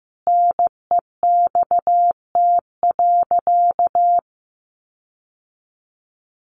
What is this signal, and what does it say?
Morse code, heard as a keyed tone, about 15 wpm, 700 Hz. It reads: NEXT.
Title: Sentences from Top 1500 Words 15wpm